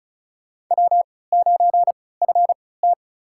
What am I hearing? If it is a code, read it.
Morse code, heard as a keyed tone, about 34 wpm, spaced as for 20 wpm, 700 Hz. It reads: W9FT
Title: USA Call Signs - 4 Characters - 3x Character Spacing - 35wpm